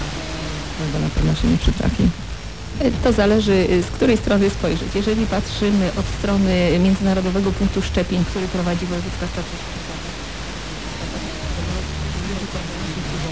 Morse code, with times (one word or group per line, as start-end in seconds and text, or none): none